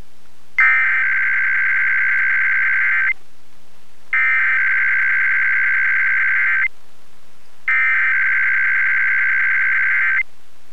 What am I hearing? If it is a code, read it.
Emergency Alert System (SAME) tones (header bursts), heard as harsh buzzing bursts. It is heard from 0.6 to 10.2 s.